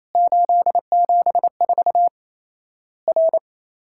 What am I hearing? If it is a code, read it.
Morse code, heard as a keyed tone, about 27 wpm, 700 Hz. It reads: 874 R
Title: Worked All Europe DX Contest - QTC Exchange - 28wpm